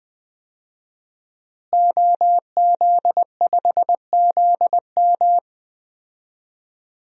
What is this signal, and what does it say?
Morse code, heard as a keyed tone, about 20 wpm, 700 Hz. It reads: OZ5ZM
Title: DXCC-100 Call Signs - 5 Characters - Repeated 3x - 20wpm